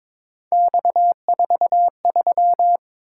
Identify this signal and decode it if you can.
Morse code, heard as a keyed tone, about 22 wpm, 700 Hz. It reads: X43
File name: practice / 22wpm / Three Letter-Number Combinations - 22wpm